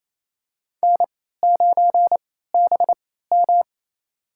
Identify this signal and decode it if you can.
Morse code, heard as a keyed tone, about 28 wpm, spaced as for 16 wpm, 700 Hz. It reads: N9BM
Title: USA Call Signs - 4 Characters - 3x Character Spacing - 28wpm